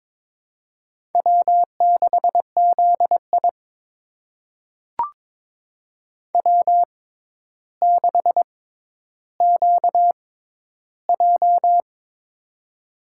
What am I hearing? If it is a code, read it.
Morse code, heard as a keyed tone, about 22 wpm, spaced as for 11 wpm, 700 Hz. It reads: W6ZI W6QJ
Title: USA Call Signs - 4 Characters - 6x Character Spacing - 22wpm